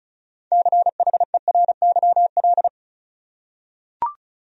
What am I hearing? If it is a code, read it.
Morse code, heard as a keyed tone, about 36 wpm, 700 Hz. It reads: CHERYL
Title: USA Common Names - 35wpm